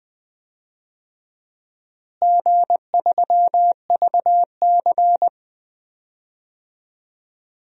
Morse code, heard as a keyed tone, about 21 wpm, 700 Hz. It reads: G3VC
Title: DXCC-100 Call Signs - 4 Characters - Repeated 2x - 20wpm